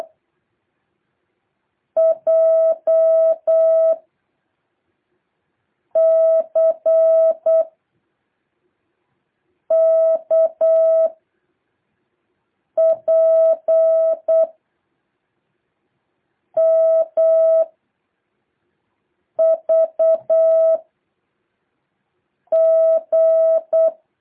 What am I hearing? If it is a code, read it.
Morse code, heard as a keyed tone, about 8 wpm, 660 Hz. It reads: J C K P M V G
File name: jh8cbh / 2023cw